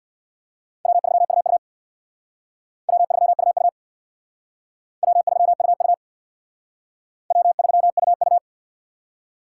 Morse code, heard as a keyed tone, about 63 wpm, 700 Hz. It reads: W3UU W3UU W3UU W3UU